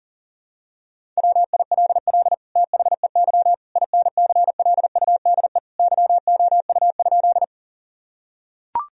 Morse code, heard as a keyed tone, about 37 wpm, 700 Hz. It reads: WILL THEY INCLUDE YOU?
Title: Sentences from Top 1000 Words - Review Part 3 - 40wpm